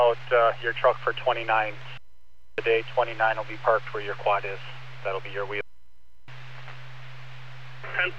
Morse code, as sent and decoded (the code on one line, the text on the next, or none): none